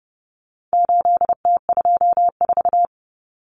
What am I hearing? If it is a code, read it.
Morse code, heard as a keyed tone, about 30 wpm, 700 Hz. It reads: 8T24
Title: Common Cut Numbers - 4-Digit 30wpm